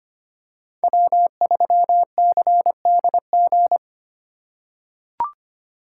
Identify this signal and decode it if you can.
Morse code, heard as a keyed tone, about 25 wpm, 700 Hz. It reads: W3CDG
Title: USA Call Signs - 5 Characters - 4x Character Spacing - 25wpm